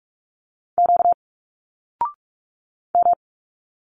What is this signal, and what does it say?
Morse code, heard as a keyed tone, about 46 wpm, 700 Hz. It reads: Q M
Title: Single Letters 45wpm